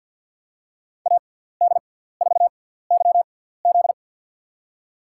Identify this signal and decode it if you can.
Morse code, heard as a keyed tone, about 47 wpm, spaced as for 17 wpm, 700 Hz. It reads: AD4YZ
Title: USA Call Signs - 5 Characters - 6x Character Spacing - 50wpm